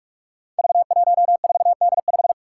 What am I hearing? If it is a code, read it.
Morse code, heard as a keyed tone, about 44 wpm, 700 Hz. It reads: V14D5
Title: Five Challenging Letter-Number Combinations 45wpm